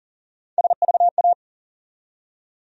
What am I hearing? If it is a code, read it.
Morse code, heard as a keyed tone, about 40 wpm, 700 Hz. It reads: SVA
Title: USA Call Sign Postfixes - 3 Letters 40wpm